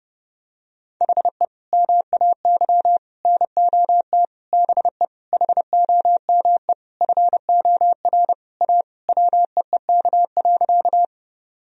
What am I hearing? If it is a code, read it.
Morse code, heard as a keyed tone, about 30 wpm, 700 Hz. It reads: HE MAY NOT BE HOME FOR A WEEK.
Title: Sentences from Top 1000 Words - Review Part 2 - 30wpm